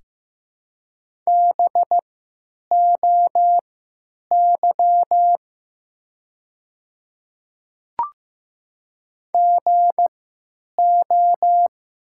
Morse code, heard as a keyed tone, about 15 wpm, spaced as for 8 wpm, 700 Hz. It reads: BOY GO
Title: 3x Character Spacing - Top 200 Words - 15wpm